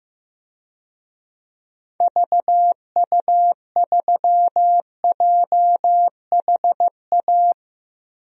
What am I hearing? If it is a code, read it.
Morse code, heard as a keyed tone, about 16 wpm, 700 Hz. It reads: VU3JHA
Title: DXCC-100 Call Signs - 6 Characters - Repeated 2x - 15wpm